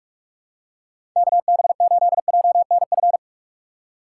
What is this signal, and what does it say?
Morse code, heard as a keyed tone, about 44 wpm, 700 Hz. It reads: KB8JNF
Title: USA Call Signs - 6 Characters - 6x Character Spacing - 45wpm